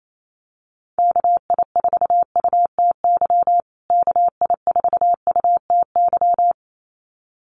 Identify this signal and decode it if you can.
Morse code, heard as a keyed tone, about 28 wpm, 700 Hz. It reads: KI4UTY KI4UTY
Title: USA Call Signs Repeated 2x 28wpm